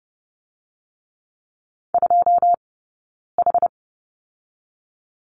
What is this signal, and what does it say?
Morse code, heard as a keyed tone, about 30 wpm, 700 Hz. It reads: 2 H
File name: Lesson 17d - Letter u - Two Characters 30wpm